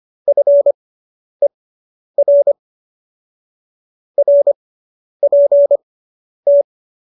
Morse code, heard as a keyed tone, about 25 wpm, spaced as for 10 wpm, 570 Hz. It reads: FER RPT
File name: QSO220 10